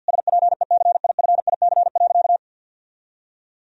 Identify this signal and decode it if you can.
Morse code, heard as a keyed tone, about 50 wpm, 700 Hz. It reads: SPECIFIC.
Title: Sentences from Top 1000 Words - Review Part 3 - 50wpm